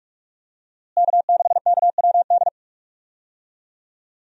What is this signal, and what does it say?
Morse code, heard as a keyed tone, about 46 wpm, 700 Hz. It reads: K6KWD
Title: USA Call Signs - 5 Characters - Repeated 3x - 45wpm